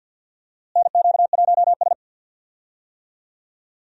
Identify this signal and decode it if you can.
Morse code, heard as a keyed tone, about 47 wpm, 700 Hz. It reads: NQ1S
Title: USA Call Signs - 4 Characters - Repeated 2x - 50wpm